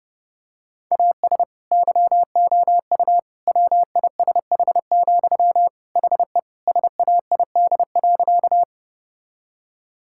Morse code, heard as a keyed tone, about 30 wpm, 700 Hz. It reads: AS YOU WISH, HE SAID.
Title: Sentences from Top 1600 Words 30wpm